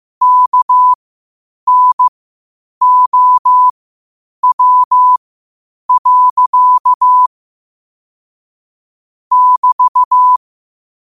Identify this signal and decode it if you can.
Morse code, heard as a keyed tone, about 15 wpm, spaced as for 8 wpm, 1000 Hz. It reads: KNOW. <BT>
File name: qotdfiles / 8wpm